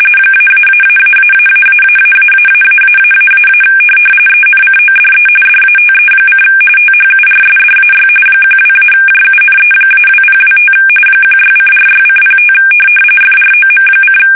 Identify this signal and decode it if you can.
RTTY, about 45.45 baud. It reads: -------------------- ENCODE DATE: THURSDAY 12-JUNE-2025 16:46:05 FROM:
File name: fsk850 / USS-Massachusetts